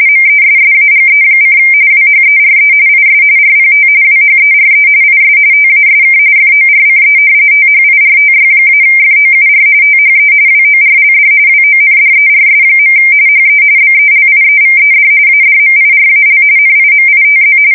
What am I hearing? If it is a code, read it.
RTTY, about 45.45 baud. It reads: MESSAGE: WHERE IS RPT WHERE IS TASK FORCE THIRTY-FOUR THE WORLD WONDERS. HE BELIEVED ADMIRAL NIMITZ
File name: fsk170 / battleshipnj